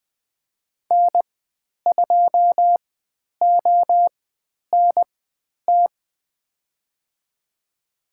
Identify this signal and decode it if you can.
Morse code, heard as a keyed tone, about 20 wpm, spaced as for 10 wpm, 700 Hz. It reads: N2ONT
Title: 095 Lesson 16g - Letter c - Call Signs 20wpm